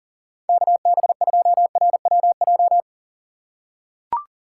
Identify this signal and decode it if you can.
Morse code, heard as a keyed tone, about 40 wpm, 700 Hz. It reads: KB2RWJ
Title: USA Call Signs - 6 Characters - 6x Character Spacing - 40wpm